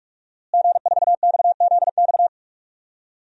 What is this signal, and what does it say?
Morse code, heard as a keyed tone, about 46 wpm, 700 Hz. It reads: G4XZX
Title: DXCC-100 Call Signs - 5 Characters - Repeated 3x - 45wpm